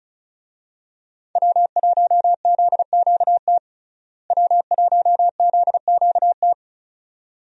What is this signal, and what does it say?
Morse code, heard as a keyed tone, about 36 wpm, 700 Hz. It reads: W1ZQT W1ZQT
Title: USA Call Signs - 5 Characters - Repeated 2x - 35wpm